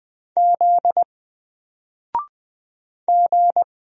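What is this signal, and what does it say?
Morse code, heard as a keyed tone, about 20 wpm, 700 Hz. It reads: Z G
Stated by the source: Single letter and number Morse drill at 20 wpm, from N4LCD.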